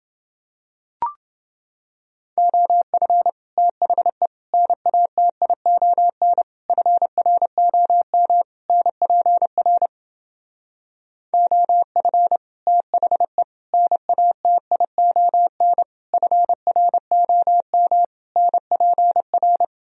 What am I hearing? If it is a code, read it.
Morse code, heard as a keyed tone, about 28 wpm, 700 Hz. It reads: OF THE NATION FROM NPR OF THE NATION FROM NPR